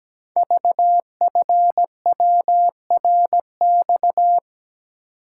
Morse code, heard as a keyed tone, about 17 wpm, 700 Hz. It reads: VFWRX
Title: Code Groups - 5 Letters - 17wpm